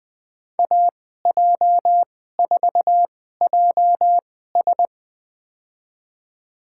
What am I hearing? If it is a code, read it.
Morse code, heard as a keyed tone, about 20 wpm, spaced as for 14 wpm, 700 Hz. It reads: AJ4JS
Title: USA Call Signs - 5 Characters - 2x Character Spacing - 20wpm